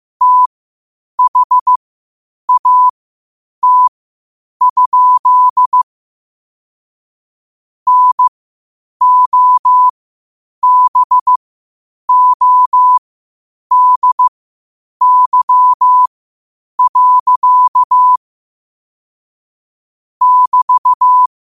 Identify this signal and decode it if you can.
Morse code, heard as a keyed tone, about 15 wpm, spaced as for 8 wpm, 1000 Hz. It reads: THAT? NOBODY. <BT>
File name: qotdfiles / 8wpm